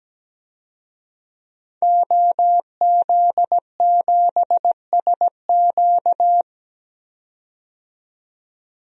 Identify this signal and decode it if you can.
Morse code, heard as a keyed tone, about 17 wpm, 700 Hz. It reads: OZ7SQ
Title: DXCC-100 Call Signs - 5 Characters - 17wpm